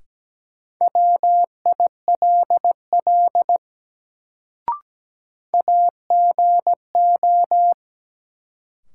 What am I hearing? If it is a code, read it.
Morse code, heard as a keyed tone, about 17 wpm, 700 Hz. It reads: WILL AGO
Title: Sets of 2 Words - New Word Drill - 17wpm